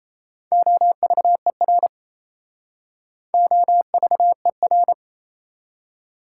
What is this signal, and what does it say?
Morse code, heard as a keyed tone, about 34 wpm, 700 Hz. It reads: OVER OVER
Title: Speed-Racing Prefixes and Suffixes 28wpm